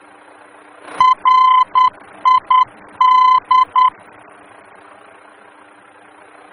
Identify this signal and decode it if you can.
Morse code, heard as a keyed tone, about 10 wpm, 1020 Hz. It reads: RID
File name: germany / mx / RID-D-VOR